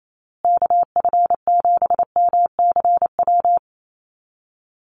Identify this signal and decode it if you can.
Morse code, heard as a keyed tone, about 28 wpm, 700 Hz. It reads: KF7MCW
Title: USA Call Signs Repeated 2x 28wpm